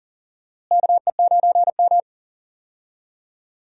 Morse code, heard as a keyed tone, about 40 wpm, 700 Hz. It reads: KE9M
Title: USA Call Signs - 4 Characters - Repeated 3x - 40wpm